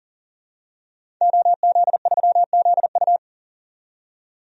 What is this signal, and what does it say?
Morse code, heard as a keyed tone, about 40 wpm, 700 Hz. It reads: OZ3ZU
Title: DXCC-100 Call Signs - 5 Characters - 40wpm